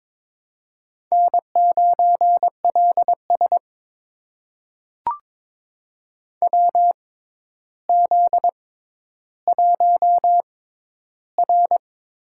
Morse code, heard as a keyed tone, about 22 wpm, spaced as for 11 wpm, 700 Hz. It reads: N9LS WZ1R
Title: USA Call Signs - 6x Character Spacing - 22wpm